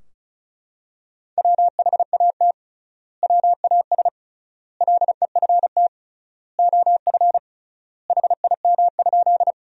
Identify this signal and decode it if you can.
Morse code, heard as a keyed tone, about 36 wpm, 700 Hz. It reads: WHAT WAS LEFT OF HIM?
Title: 3x Word-Spacing - Sentences from Top 300 Words - 35wpm